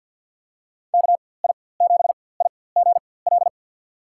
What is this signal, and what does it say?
Morse code, heard as a keyed tone, about 53 wpm, spaced as for 24 wpm, 700 Hz. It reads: KI7IGL
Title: USA Call Signs - 6 Characters - 4x Character Spacing - 50wpm